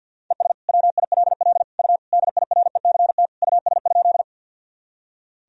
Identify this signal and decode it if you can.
Morse code, heard as a keyed tone, about 46 wpm, 700 Hz. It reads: ES WILL U DIRECT US?
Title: Sentences from Top 400 Words and Abbreviations 50wpm